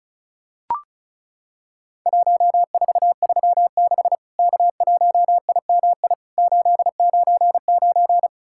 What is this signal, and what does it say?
Morse code, heard as a keyed tone, about 36 wpm, 700 Hz. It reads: 1436 K1IMI 899
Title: Worked All Europe DX Contest - QTC Exchange - 35wpm